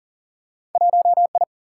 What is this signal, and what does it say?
Morse code, heard as a keyed tone, about 40 wpm, 700 Hz. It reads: 1I
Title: Single-Letter Single-Number 40wpm